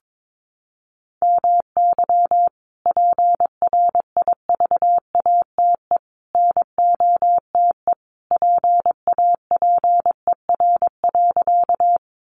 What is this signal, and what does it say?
Morse code, heard as a keyed tone, about 22 wpm, 700 Hz. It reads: MY PRIVATE NOTE PAPER.
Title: Sentences from Top 1000 Words 22wpm